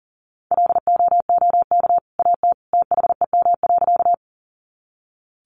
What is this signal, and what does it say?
Morse code, heard as a keyed tone, about 40 wpm, 700 Hz. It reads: LOOK AT THEM.
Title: Sentences from Top 100 Words 40wpm